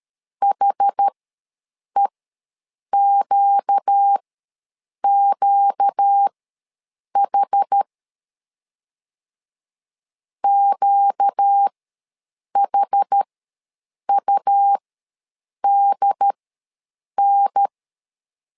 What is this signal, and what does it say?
Morse code, heard as a keyed tone, about 13 wpm, spaced as for 7 wpm, 800 Hz. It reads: HEQQH QHUDN
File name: LEZIONI / 4 Q ws 12wpm 1912